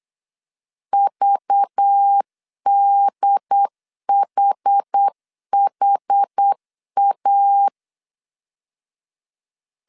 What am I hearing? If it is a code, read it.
Morse code, heard as a keyed tone, about 8 wpm, 800 Hz. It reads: VDHHA
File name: CW / DET / 1 6 1500C S C 8wpm 5219